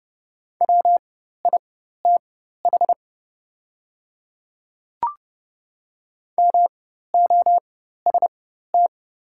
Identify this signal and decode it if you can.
Morse code, heard as a keyed tone, about 30 wpm, spaced as for 14 wpm, 690 Hz. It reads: WITH MOST
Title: 4x Character Spacing - Top 100 Words - 30wpm